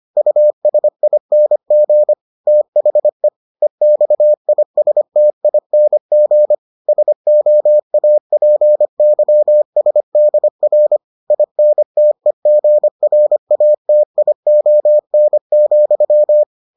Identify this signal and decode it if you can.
Morse code, heard as a keyed tone, about 25 wpm, 600 Hz. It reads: USING THE EXISTING SOAPYSDR INTEGRATION,